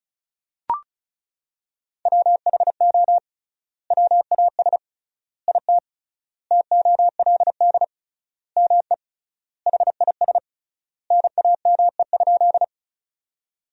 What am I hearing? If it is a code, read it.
Morse code, heard as a keyed tone, about 34 wpm, 700 Hz. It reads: WHO WAS IT TOLD ME HIS NAME?
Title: 3x Word-Spacing - Sentences from Top 400 Words - 35wpm